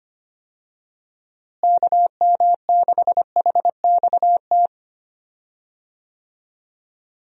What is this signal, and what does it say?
Morse code, heard as a keyed tone, about 25 wpm, 700 Hz. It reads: KM6HXT
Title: USA Call Signs - 6 Characters - Repeated 2x - 25wpm